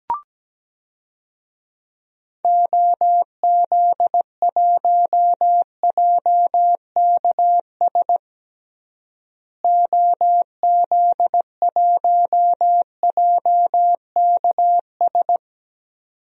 Morse code, heard as a keyed tone, about 17 wpm, 700 Hz. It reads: OZ1JKS OZ1JKS
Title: DXCC-100 Call Signs - 6 Characters - Repeated 2x - 17wpm